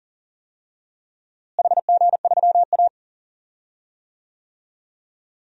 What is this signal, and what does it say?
Morse code, heard as a keyed tone, about 40 wpm, 700 Hz. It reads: HG3A
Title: DXCC-100 Call Signs - 4 Characters - Repeated 3x - 40wpm